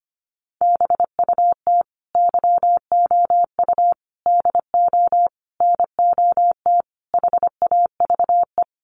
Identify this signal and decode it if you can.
Morse code, heard as a keyed tone, about 25 wpm, 690 Hz. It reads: BUT YOU DO NOT HAVE
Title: Sets of 5 Words Encore 25wpm